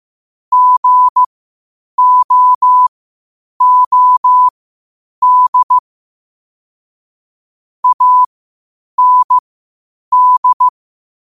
Morse code, heard as a keyed tone, about 15 wpm, spaced as for 8 wpm, 1000 Hz. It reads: GOOD AND
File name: qotdfiles / 8wpm